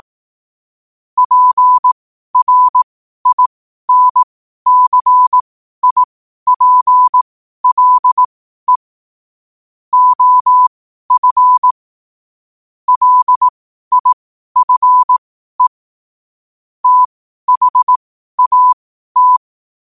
Morse code, heard as a keyed tone, about 18 wpm, spaced as for 12 wpm, 1000 Hz. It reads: PRINCIPLE OF LIFE THAT
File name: qotdfiles / 12wpm